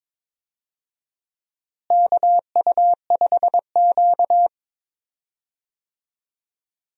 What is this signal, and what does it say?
Morse code, heard as a keyed tone, about 22 wpm, 700 Hz. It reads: KU5Q
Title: USA Call Signs - 4 Characters - Repeated 3x - 22wpm